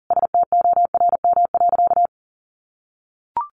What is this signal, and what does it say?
Morse code, heard as a keyed tone, about 40 wpm, 700 Hz. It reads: STORM.
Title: Sentences from Top 1500 Words 40wpm